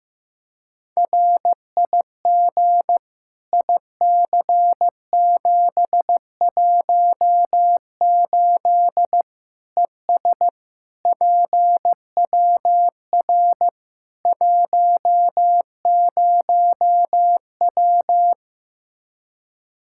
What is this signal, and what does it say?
Morse code, heard as a keyed tone, about 15 wpm, 700 Hz. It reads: RIG IC718 ES PWR 10W
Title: Practice QSO - Exchange 2 - Info - 15wpm